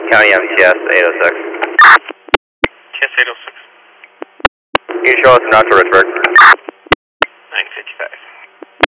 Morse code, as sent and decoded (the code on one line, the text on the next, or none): none